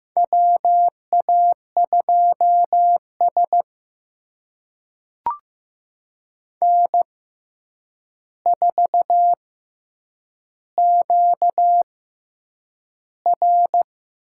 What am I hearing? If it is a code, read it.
Morse code, heard as a keyed tone, about 15 wpm, spaced as for 8 wpm, 700 Hz. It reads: WA2S N4QR
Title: USA Call Signs - 4 Characters - 6x Character Spacing - 15wpm